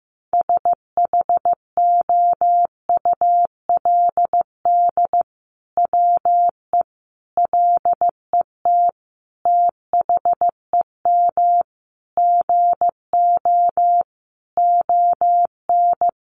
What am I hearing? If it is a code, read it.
Morse code, heard as a keyed tone, about 15 wpm, 700 Hz. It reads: SHOULD WE LET THEM GO ON
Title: Sentences from Top 600 Words 15wpm